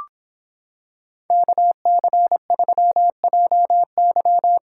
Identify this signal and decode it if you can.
Morse code, heard as a keyed tone, about 26 wpm, 700 Hz. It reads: KC3JY